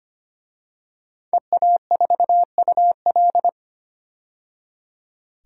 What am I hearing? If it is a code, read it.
Morse code, heard as a keyed tone, about 25 wpm, 700 Hz. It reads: EA4UL
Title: DXCC-100 Call Signs - 5 Characters - 25wpm